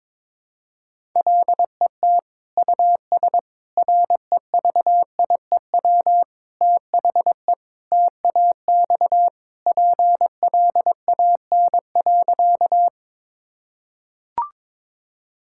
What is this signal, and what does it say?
Morse code, heard as a keyed tone, about 21 wpm, 700 Hz. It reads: LET US REVIEW THE TAX PLAN.
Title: Sentences from Top 1000 Words - Review Part 6 - 22wpm